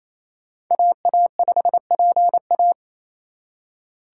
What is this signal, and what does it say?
Morse code, heard as a keyed tone, about 35 wpm, 700 Hz. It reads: AA5PA
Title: Speed Racing - USA 2x2 Call Signs 28wpm